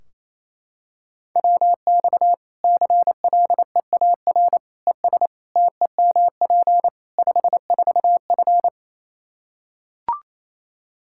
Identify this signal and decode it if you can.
Morse code, heard as a keyed tone, about 28 wpm, 700 Hz. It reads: WX CLEAR ES TEMP 54F
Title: Practice QSO - Exchange 2 - Info - 28wpm